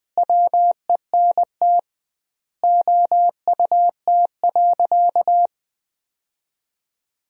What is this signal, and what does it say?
Morse code, heard as a keyed tone, about 21 wpm, 700 Hz. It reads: WENT OUT.
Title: 2x Word-Spacing - Sentences from Top 300 Words - 20wpm